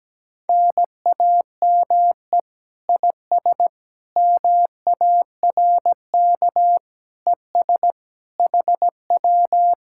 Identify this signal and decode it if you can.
Morse code, heard as a keyed tone, about 17 wpm, 700 Hz. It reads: NAME IS MARK ES HW
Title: Speed-Racing Common QSO Phrases Encore - 17wpm